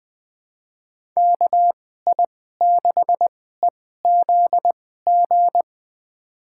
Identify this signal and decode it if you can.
Morse code, heard as a keyed tone, about 20 wpm, spaced as for 14 wpm, 700 Hz. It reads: KI6EZG
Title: USA Call Signs - 6 Characters - 2x Character Spacing - 20wpm